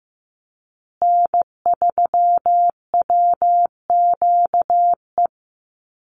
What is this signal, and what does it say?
Morse code, heard as a keyed tone, about 15 wpm, 700 Hz. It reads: N3WQE
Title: USA Call Signs Repeated 2x 15wpm